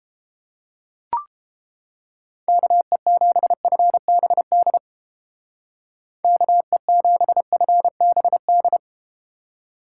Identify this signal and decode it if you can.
Morse code, heard as a keyed tone, about 30 wpm, 700 Hz. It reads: KE7FBD KE7FBD